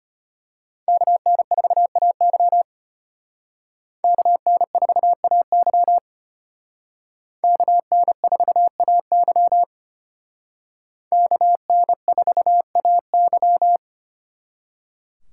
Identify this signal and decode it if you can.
Morse code, heard as a keyed tone, about 32 wpm, 700 Hz. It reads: KN4AY KN4AY KN4AY KN4AY